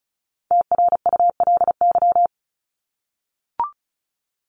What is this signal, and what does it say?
Morse code, heard as a keyed tone, about 34 wpm, 700 Hz. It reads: TRULY
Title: Top 1501-1750 Words 35wpm